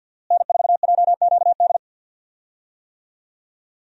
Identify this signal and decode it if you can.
Morse code, heard as a keyed tone, about 51 wpm, 700 Hz. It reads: N4JQD
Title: USA Call Signs - 5 Characters - Repeated 3x - 50wpm